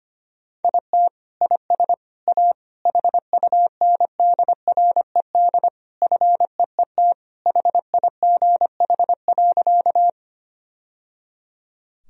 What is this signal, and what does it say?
Morse code, heard as a keyed tone, about 25 wpm, 700 Hz. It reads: IT IS A HUNDRED FEET HIGH.
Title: Sentences from Top 1000 Words - Review Part 2 - 25wpm